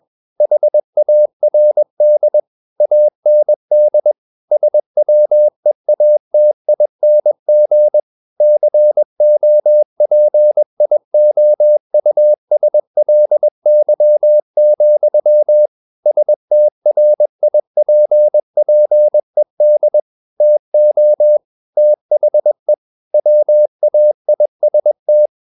Morse code, heard as a keyed tone, about 21 wpm, 600 Hz. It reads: HARD AND SWEATING COPIOUSLY, STRIPPED TO THE WAIST
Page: CW med Gnister